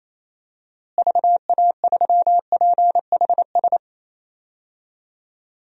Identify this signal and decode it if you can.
Morse code, heard as a keyed tone, about 28 wpm, 700 Hz. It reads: VA3PHS
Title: DXCC-100 Call Signs - 6 Characters - Repeated 3x - 28wpm